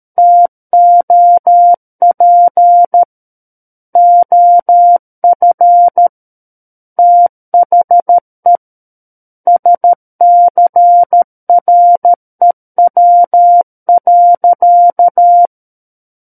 Morse code, heard as a keyed tone, about 13 wpm, 700 Hz. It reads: TOP OF THE SCREW.
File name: Book One / 04-Chapter 4 - 13 WPM